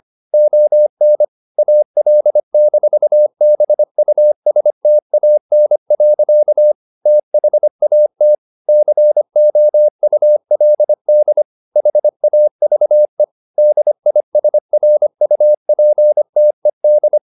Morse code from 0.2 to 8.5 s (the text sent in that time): ON AL-BUSTAN. THAT